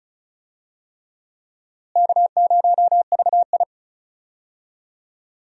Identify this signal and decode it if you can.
Morse code, heard as a keyed tone, about 34 wpm, 700 Hz. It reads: K0VI
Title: USA Call Signs - 4 Characters - Repeated 3x - 35wpm